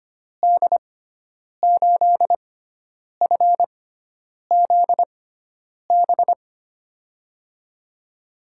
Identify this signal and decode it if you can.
Morse code, heard as a keyed tone, about 25 wpm, spaced as for 9 wpm, 700 Hz. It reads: D8FZB
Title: USA Call Signs - 6 Characters - 6x Character Spacing - 25wpm